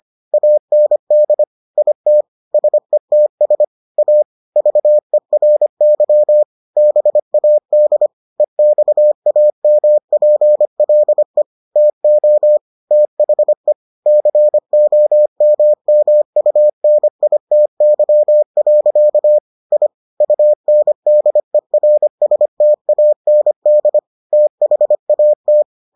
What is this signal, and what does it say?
Morse code, heard as a keyed tone, about 25 wpm, 600 Hz. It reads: AND IT SETS A VERY BAD EXAMPLE TO THE COMMUNITY. I UNDERSTAND THAT